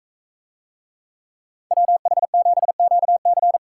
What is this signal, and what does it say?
Morse code, heard as a keyed tone, about 42 wpm, 700 Hz. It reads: WH7QC